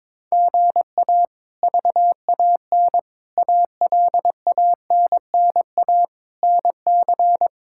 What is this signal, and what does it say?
Morse code, heard as a keyed tone, about 22 wpm, 700 Hz. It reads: GA VAN ALANNA NC
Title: K1USN SST QSO Practice - 22wpm